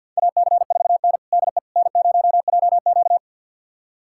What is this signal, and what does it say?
Morse code, heard as a keyed tone, about 45 wpm, 700 Hz. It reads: AC4N DE N0JQ
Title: USA Callsign DE Callsign - 4 Characters - 50wpm